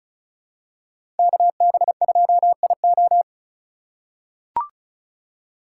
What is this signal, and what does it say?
Morse code, heard as a keyed tone, about 36 wpm, 700 Hz. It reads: KB2IO
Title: USA Call Signs - 5 Characters - 8x Character Spacing - 35wpm